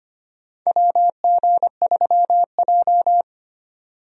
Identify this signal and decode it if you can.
Morse code, heard as a keyed tone, about 25 wpm, 700 Hz. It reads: WG3J
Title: USA Call Signs - 4 Characters - 25wpm